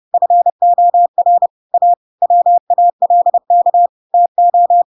Morse code, heard as a keyed tone, about 30 wpm, 700 Hz. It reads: FOR A WALK TO
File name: Pinocchio / Pinocchio ch5 30 30 ik1hge